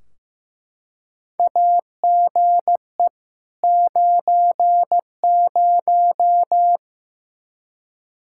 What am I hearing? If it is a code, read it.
Morse code, heard as a keyed tone, about 15 wpm, 700 Hz. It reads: AGE 90
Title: Practice QSO - Exchange 3 - More Info - 15wpm